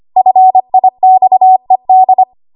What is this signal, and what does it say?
Morse code, heard as a keyed tone, about 25 wpm, 750 Hz. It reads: FIXED
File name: Morse / 25WPM